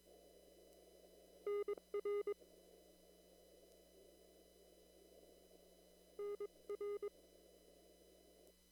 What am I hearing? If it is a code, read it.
Morse code, heard as a keyed tone, about 22 wpm, spaced as for 20 wpm, 410 Hz. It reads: NR NR
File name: CW / kt5j wpx cw-op-k5pi